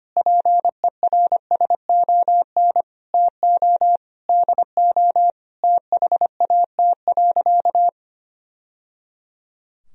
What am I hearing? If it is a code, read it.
Morse code, heard as a keyed tone, about 25 wpm, 700 Hz. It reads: PERSON TO DO THAT.
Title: Sentences from Top 1000 Words - Review Part 3 - 25wpm